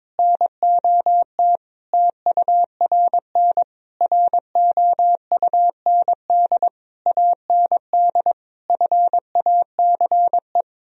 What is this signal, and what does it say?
Morse code, heard as a keyed tone, about 22 wpm, 700 Hz. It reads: NOT TURN ROUND AND FACE
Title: Sentences from Top 1000 Words - Review Part 2 - 22wpm